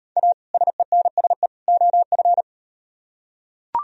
Morse code, heard as a keyed tone, about 35 wpm, 700 Hz. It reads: A SENSE OF